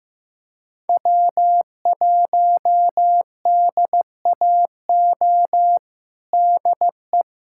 Morse code, heard as a keyed tone, about 16 wpm, 700 Hz. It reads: W1DAO DE
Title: USA Callsign DE Callsign - 5 Characters - 15wpm